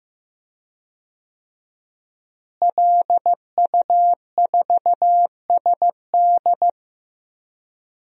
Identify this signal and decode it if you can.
Morse code, heard as a keyed tone, about 15 wpm, 700 Hz. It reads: LU4SD
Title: DXCC-100 Call Signs - 5 Characters - Repeated 2x - 15wpm